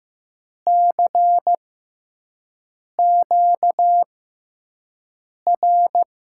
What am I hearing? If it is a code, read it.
Morse code, heard as a keyed tone, about 15 wpm, spaced as for 5 wpm, 700 Hz. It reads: CQR
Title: USA Call Signs - 6x Character Spacing - 15wpm